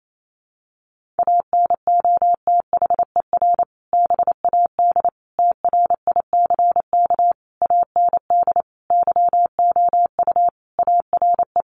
Morse code, heard as a keyed tone, about 28 wpm, 700 Hz. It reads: ANOTHER BAD TRICK AND YOU ARE
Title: Sentences from Top 1500 Words 28wpm